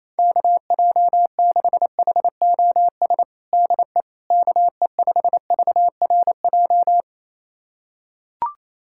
Morse code, heard as a keyed tone, about 28 wpm, 700 Hz. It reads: KJ6HOS DE KE5VRJ
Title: USA Callsign DE Callsign - 6 Characters - 28wpm